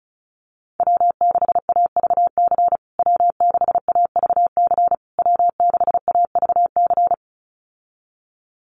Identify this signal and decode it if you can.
Morse code, heard as a keyed tone, about 34 wpm, 700 Hz. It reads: W6AVC W6AVC W6AVC
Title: USA Call Signs Repeated 3x 35wpm